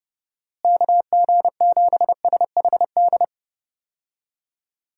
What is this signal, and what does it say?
Morse code, heard as a keyed tone, about 31 wpm, 700 Hz. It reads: KG7SHD
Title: USA Call Signs - 6 Characters - Repeated 2x - 30wpm